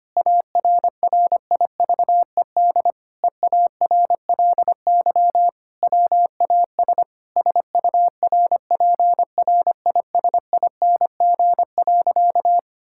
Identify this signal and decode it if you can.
Morse code, heard as a keyed tone, about 25 wpm, 700 Hz. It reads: ARRIVED EARLY WAS SURPRISING.
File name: practice / 25wpm / Sentences with 4-Word Phrases - 25wpm